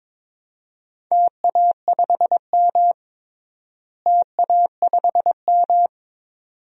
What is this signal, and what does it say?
Morse code, heard as a keyed tone, about 23 wpm, 700 Hz. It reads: TA5M TA5M
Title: DXCC-100 Call Signs - 4 Characters - Repeated 2x - 22wpm